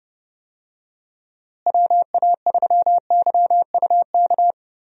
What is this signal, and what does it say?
Morse code, heard as a keyed tone, about 30 wpm, 700 Hz. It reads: WA3YUK
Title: USA Call Signs - 6 Characters - Repeated 2x - 30wpm